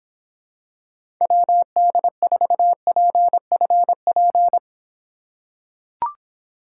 Morse code, heard as a keyed tone, about 26 wpm, 700 Hz. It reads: WD4PFP